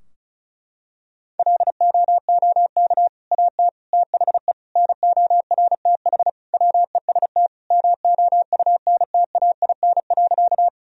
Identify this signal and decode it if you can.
Morse code, heard as a keyed tone, about 34 wpm, 700 Hz. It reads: LOOK AT THE NORTH WEST MOUNTAIN.
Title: Sentences from Top 1000 Words - Review Part 2 - 35wpm